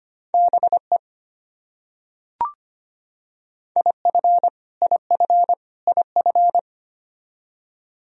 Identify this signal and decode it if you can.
Morse code, heard as a keyed tone, about 25 wpm, 700 Hz. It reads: BE IF IF IF
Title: Top 100 Words Repeated 25wpm